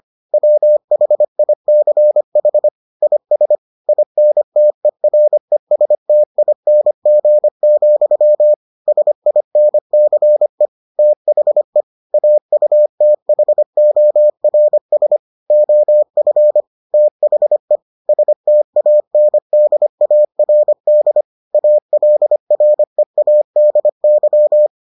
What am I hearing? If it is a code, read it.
Morse code, heard as a keyed tone, about 25 wpm, 600 Hz. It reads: WHICH IS INTERESTING, SINCE THE AUTHORS OF THE STANDARD ALREADY